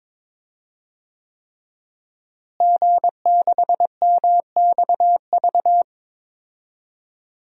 Morse code, heard as a keyed tone, about 23 wpm, 700 Hz. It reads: G6MXV
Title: DXCC-100 Call Signs - 5 Characters - Repeated 3x - 22wpm